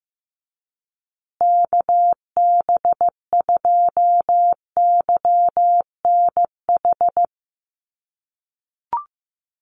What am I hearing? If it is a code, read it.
Morse code, heard as a keyed tone, about 15 wpm, 700 Hz. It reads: KB2YNH
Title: USA Call Signs Repeated 3x 15wpm